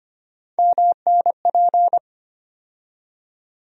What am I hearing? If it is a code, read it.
Morse code, heard as a keyed tone, about 25 wpm, 700 Hz. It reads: MNP
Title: USA Call Sign Postfixes - 3 Letters 25wpm